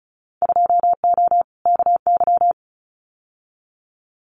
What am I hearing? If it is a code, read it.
Morse code, heard as a keyed tone, about 36 wpm, 700 Hz. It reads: 2O KY
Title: Winter Field Day Exchanges 35wpm